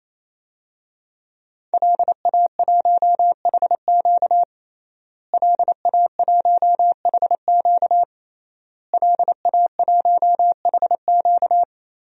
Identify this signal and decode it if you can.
Morse code, heard as a keyed tone, about 28 wpm, 700 Hz. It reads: LA1HQ LA1HQ LA1HQ
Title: DXCC-100 Call Signs - 5 Characters - Repeated 3x - 28wpm